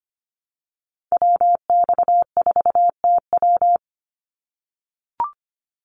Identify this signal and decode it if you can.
Morse code, heard as a keyed tone, about 25 wpm, 700 Hz. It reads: WX4TW
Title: NA SOTA Activator Callsigns 25wpm